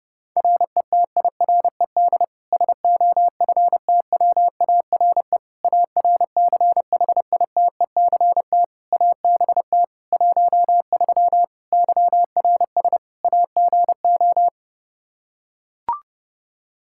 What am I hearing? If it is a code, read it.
Morse code, heard as a keyed tone, about 30 wpm, 700 Hz. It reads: RETIRED SOFTWARE ARCHITECT ABT 13 YRS AGO
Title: Practice QSO - Exchange 3 - More Info - 30wpm